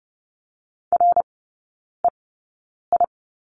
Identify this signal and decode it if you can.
Morse code, heard as a keyed tone, about 30 wpm, spaced as for 9 wpm, 700 Hz. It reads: REI
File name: Lesson 18e - Letter m - Three Characters 30wpm